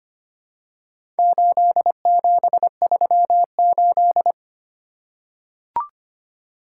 Morse code, heard as a keyed tone, about 25 wpm, 700 Hz. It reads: 8738
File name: practice / 25wpm / Four Letter-Number Combinations - 25wpm